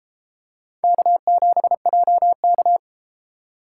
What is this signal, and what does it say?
Morse code, heard as a keyed tone, about 32 wpm, 700 Hz. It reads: K7JK